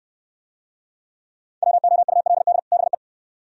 Morse code, heard as a keyed tone, about 68 wpm, 700 Hz. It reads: WOULD BE